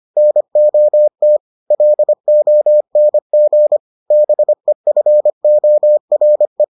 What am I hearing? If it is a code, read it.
Morse code, heard as a keyed tone, about 25 wpm, 600 Hz. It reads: NOT LONG BEFORE